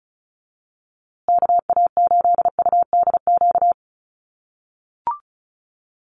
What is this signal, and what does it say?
Morse code, heard as a keyed tone, about 34 wpm, 700 Hz. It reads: KA8UDQ
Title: USA Call Signs Repeated 3x 35wpm